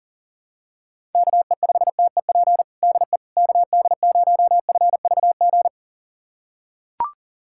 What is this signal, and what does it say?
Morse code, heard as a keyed tone, about 40 wpm, 700 Hz. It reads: KE5TEP DE KD0FVG
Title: USA Callsign DE Callsign - 6 Characters - 40wpm